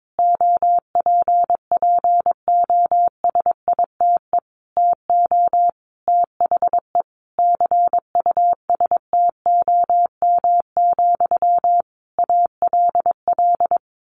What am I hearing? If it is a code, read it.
Morse code, heard as a keyed tone, about 22 wpm, 700 Hz. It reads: OPPOSITE TO THE CUSTOM, ALL
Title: A Sound Of Thunder 22wpm